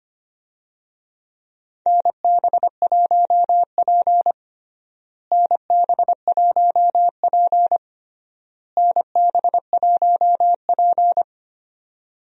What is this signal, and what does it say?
Morse code, heard as a keyed tone, about 25 wpm, 700 Hz. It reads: NB1P NB1P NB1P
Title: USA Call Signs - 4 Characters - Repeated 3x - 25wpm